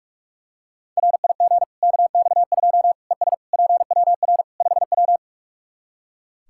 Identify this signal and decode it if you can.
Morse code, heard as a keyed tone, about 45 wpm, 700 Hz. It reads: RIG KX2 ES PWR 5W
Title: K1USN SST QSO Practice 45wpm